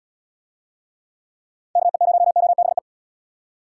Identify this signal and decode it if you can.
Morse code, heard as a keyed tone, about 78 wpm, 700 Hz. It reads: KE0OBE